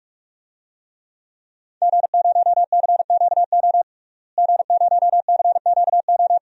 Morse code, heard as a keyed tone, about 46 wpm, 700 Hz. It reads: G0CQO G0CQO
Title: DXCC-100 Call Signs - 5 Characters - Repeated 2x - 45wpm